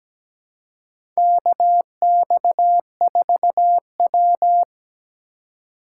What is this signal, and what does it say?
Morse code, heard as a keyed tone, about 17 wpm, 700 Hz. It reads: KX4W
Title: USA Call Signs - 4 Characters - 3x Character Spacing - 17wpm